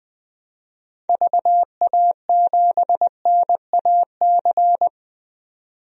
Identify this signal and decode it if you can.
Morse code, heard as a keyed tone, about 21 wpm, 700 Hz. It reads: VA7NAC
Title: DXCC-100 Call Signs - 6 Characters - 20wpm